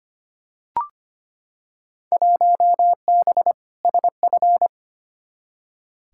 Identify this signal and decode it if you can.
Morse code, heard as a keyed tone, about 25 wpm, 700 Hz. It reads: 1B SF
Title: Field Day Exchanges Encore - 25wpm